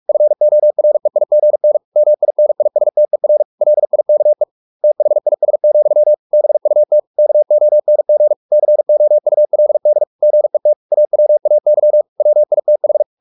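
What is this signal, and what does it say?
Morse code, heard as a keyed tone, about 44 wpm, 600 Hz. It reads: FOREIGN MINISTER LIKE THIS, BUT KONG COULD GET AWAY WITH
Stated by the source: Morse practice at 45 wpm, the novel Never by Ken Follett in English, from CW med Gnister.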